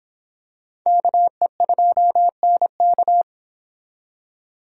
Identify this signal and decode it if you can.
Morse code, heard as a keyed tone, about 26 wpm, 700 Hz. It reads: KE2NK